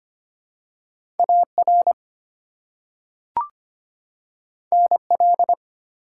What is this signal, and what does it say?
Morse code, heard as a keyed tone, about 25 wpm, 700 Hz. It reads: AR NL
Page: morse-code-ninja-practice-25wpm